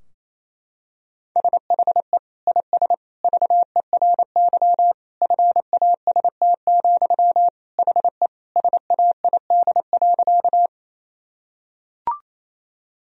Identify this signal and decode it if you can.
Morse code, heard as a keyed tone, about 28 wpm, 700 Hz. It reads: SHE IS VERY FAST, HE SAID.
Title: Sentences from Top 1000 Words - Review Part 2 - 28wpm